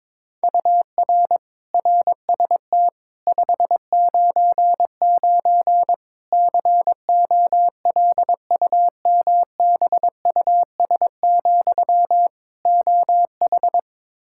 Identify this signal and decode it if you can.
Morse code, heard as a keyed tone, about 22 wpm, 700 Hz. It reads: UR RST 599 COLUMBUS, OH
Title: K1USN SST QSO Practice 22wpm